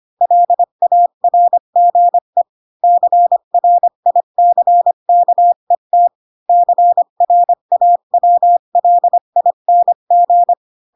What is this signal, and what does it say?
Morse code, heard as a keyed tone, about 25 wpm, 700 Hz. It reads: LARGE CRICKET CRAWLING
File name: Pinocchio / Pinocchio ch4 20 25 ik1hge